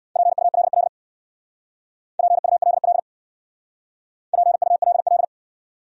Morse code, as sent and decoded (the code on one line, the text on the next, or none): .--. ..- .-.. .-.. / .--. ..- .-.. .-.. / .--. ..- .-.. .-..
PULL PULL PULL